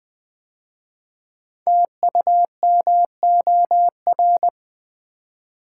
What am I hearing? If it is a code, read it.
Morse code, heard as a keyed tone, about 20 wpm, 700 Hz. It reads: TUMOR
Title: 5 Letter Words - First 3 Letters Given - 20wpm